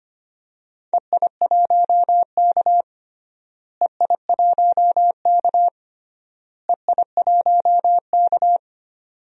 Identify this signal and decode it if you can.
Morse code, heard as a keyed tone, about 25 wpm, 700 Hz. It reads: EI1K EI1K EI1K
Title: DXCC-100 Call Signs - 4 Characters - Repeated 3x - 25wpm